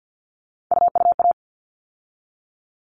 Morse code, heard as a keyed tone, about 50 wpm, 700 Hz. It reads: UUA
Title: All Cut Numbers - 3-Digit 50wpm